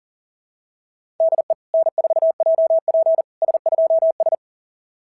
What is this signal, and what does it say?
Morse code, heard as a keyed tone, about 40 wpm, 690 Hz. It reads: DE N4JP S2S
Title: SOTA QSO Practice 40wpm